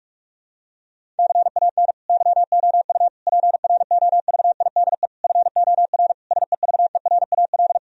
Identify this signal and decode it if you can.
Morse code, heard as a keyed tone, about 46 wpm, 700 Hz. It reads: CAN YOU PROVIDE FOR SEVERAL
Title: Sentences from Top 1000 Words - Review Part 2 - 45wpm